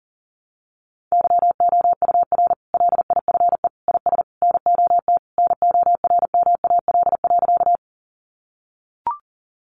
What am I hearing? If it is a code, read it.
Morse code, heard as a keyed tone, about 40 wpm, 700 Hz. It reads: YOUR LIFE IS NOT NORMAL.
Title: Sentences from Top 1200 Words 40wpm